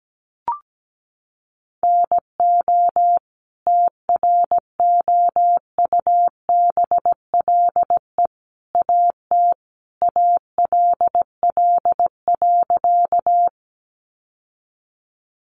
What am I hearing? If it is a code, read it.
Morse code, heard as a keyed tone, about 17 wpm, 700 Hz. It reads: NO TROUBLE AT ALL.
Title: Sentences from Top 1200 Words 17wpm